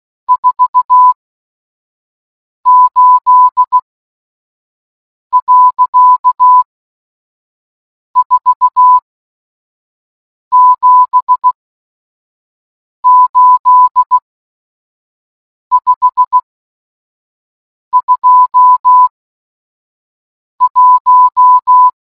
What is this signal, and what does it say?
Morse code, heard as a keyed tone, about 16 wpm, spaced as for 5 wpm, 1000 Hz. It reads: 48.478521